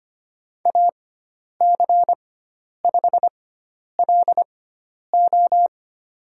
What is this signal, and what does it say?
Morse code, heard as a keyed tone, about 25 wpm, spaced as for 10 wpm, 700 Hz. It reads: AC5LO
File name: Lesson 16g - Letter c - Call Signs 25wpm